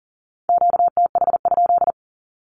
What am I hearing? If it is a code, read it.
Morse code, heard as a keyed tone, about 38 wpm, 700 Hz. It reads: QTH?
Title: Q-Codes with Spoken Meaning - 40wpm